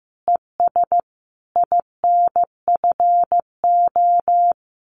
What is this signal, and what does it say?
Morse code, heard as a keyed tone, about 15 wpm, 700 Hz. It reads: ES INFO
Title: Common QSO Phrases Encore 15wpm